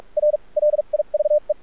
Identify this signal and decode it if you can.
Morse code, heard as a keyed tone, about 46 wpm, spaced as for 37 wpm, 600 Hz. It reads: R LIVE
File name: archived / CRSnet